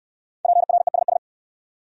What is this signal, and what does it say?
Morse code, heard as a keyed tone, about 62 wpm, 690 Hz. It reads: PRESS